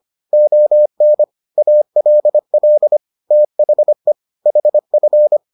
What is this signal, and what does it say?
Morse code, heard as a keyed tone, about 25 wpm, 600 Hz. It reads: ON ALL THE HF